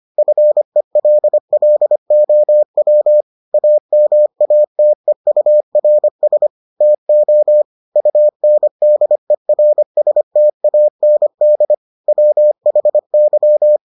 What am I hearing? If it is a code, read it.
Morse code, heard as a keyed tone, about 25 wpm, 600 Hz. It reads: FELLOW AMATEURS TO UNDERSTAND WHY